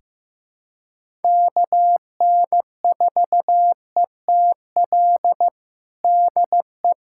Morse code, heard as a keyed tone, about 15 wpm, 700 Hz. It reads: KN4ETL DE
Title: USA Callsign DE Callsign 15wpm